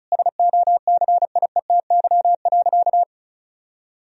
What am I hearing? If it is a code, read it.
Morse code, heard as a keyed tone, about 34 wpm, 700 Hz. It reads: SOCIETY.
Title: Sentences from Top 1000 Words - Review Part 4 - 35wpm